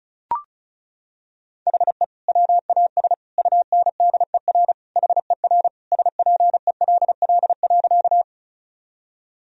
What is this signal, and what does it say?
Morse code, heard as a keyed tone, about 36 wpm, 700 Hz. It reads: HE WAS UNDER HER SPELL.
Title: Sentences from Top 1000 Words - Review Part 1 - 35wpm